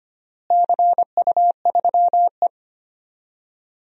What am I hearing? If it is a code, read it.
Morse code, heard as a keyed tone, about 25 wpm, 700 Hz. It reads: CU3E
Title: DXCC-100 Call Signs - 4 Characters - Repeated 3x - 25wpm